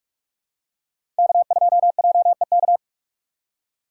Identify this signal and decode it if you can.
Morse code, heard as a keyed tone, about 46 wpm, 700 Hz. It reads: K2JEK
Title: USA Call Signs - 5 Characters - 6x Character Spacing - 45wpm